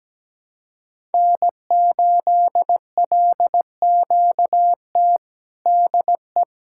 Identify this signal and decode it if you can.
Morse code, heard as a keyed tone, about 17 wpm, 700 Hz. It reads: N8LQT DE
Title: Practice QSO - Exchange 4 - Ending - 17wpm